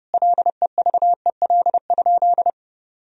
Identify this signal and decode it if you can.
Morse code, heard as a keyed tone, about 30 wpm, 700 Hz. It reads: LEVEL?
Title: Sentences from Top 1000 Words - Review Part 4 - 30wpm